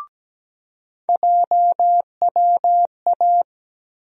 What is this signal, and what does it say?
Morse code, heard as a keyed tone, about 17 wpm, 700 Hz. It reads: JWA
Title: USA Call Sign Postfixes - 3 Letters 17wpm